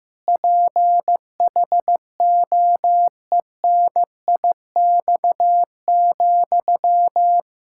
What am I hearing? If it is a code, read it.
Morse code, heard as a keyed tone, about 15 wpm, 700 Hz. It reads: PHOENIX,
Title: Practice QSO - Exchange 1 - Beginning - 15wpm